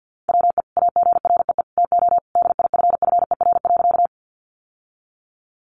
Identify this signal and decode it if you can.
Morse code, heard as a keyed tone, about 49 wpm, 700 Hz. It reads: WE AGREE TO DIFFER.
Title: Sentences from Top 1100 Words 50wpm